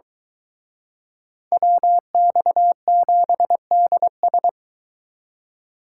Morse code, heard as a keyed tone, about 23 wpm, 700 Hz. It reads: WX7DS